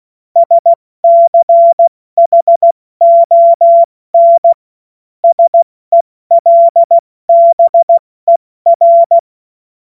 Morse code, heard as a keyed tone, about 16 wpm, 680 Hz. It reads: SCHON SELBER
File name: Witz-29-Juli-2025-70-F80